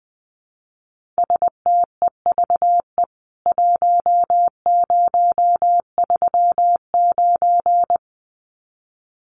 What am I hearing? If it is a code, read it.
Morse code, heard as a keyed tone, about 19 wpm, 700 Hz. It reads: STEVE 1039
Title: CWT Exchange 20wpm